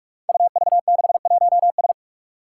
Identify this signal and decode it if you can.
Morse code, heard as a keyed tone, about 43 wpm, 700 Hz. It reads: UV61S
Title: Five Challenging Letter-Number Combinations 45wpm